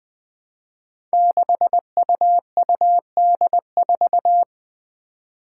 Morse code, heard as a keyed tone, about 20 wpm, 700 Hz. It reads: 6UUD4
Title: Five Challenging Letter-Number Combinations 20wpm